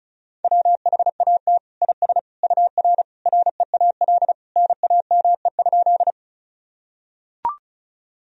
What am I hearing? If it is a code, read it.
Morse code, heard as a keyed tone, about 36 wpm, 700 Hz. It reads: WHAT IS UR REAL NAME?
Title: Sentences from Top 300 Words and Abbreviations 35wpm